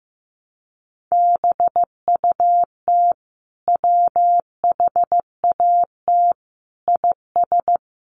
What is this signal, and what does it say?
Morse code, heard as a keyed tone, about 15 wpm, 700 Hz. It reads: BUT WHAT IS
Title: Sentences from Top 300 Words 15wpm